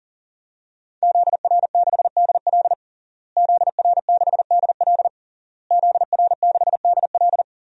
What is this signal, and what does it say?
Morse code, heard as a keyed tone, about 40 wpm, 700 Hz. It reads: ZR6DL ZR6DL ZR6DL
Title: DXCC-100 Call Signs - 5 Characters - Repeated 3x - 40wpm